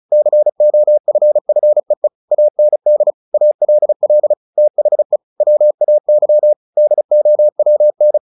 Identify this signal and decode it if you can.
Morse code, heard as a keyed tone, about 36 wpm, 600 Hz. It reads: COFFEE AND ALL THE WAY DOWN
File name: the old man and the sea 2 35wpm